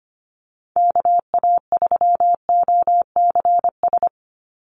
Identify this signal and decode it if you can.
Morse code, heard as a keyed tone, about 25 wpm, 700 Hz. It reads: KA3OCS
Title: Call Signs - 25wpm